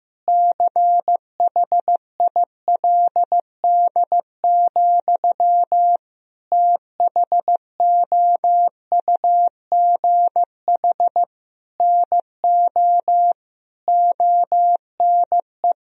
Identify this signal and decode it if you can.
Morse code, heard as a keyed tone, about 15 wpm, 700 Hz. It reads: CHILD, THOUGH NO ONE
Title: Sentences from Top 1000 Words - Review Part 2 - 15wpm